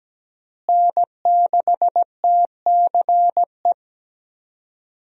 Morse code, heard as a keyed tone, about 17 wpm, 700 Hz. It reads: N6TCE
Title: USA Call Signs - 5 Characters - Repeated 3x - 17wpm